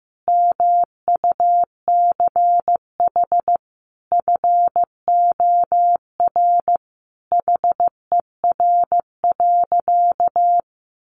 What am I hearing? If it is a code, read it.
Morse code, heard as a keyed tone, about 15 wpm, 700 Hz. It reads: MUCH FOR HER.
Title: Sentences from Top 1200 Words 15wpm